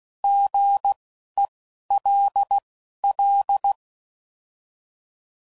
Morse code, heard as a keyed tone, about 16 wpm, spaced as for 11 wpm, 800 Hz. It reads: GELL
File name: Prefixes / By First Letter